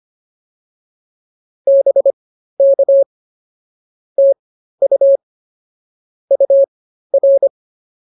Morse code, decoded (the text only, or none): BK TU UR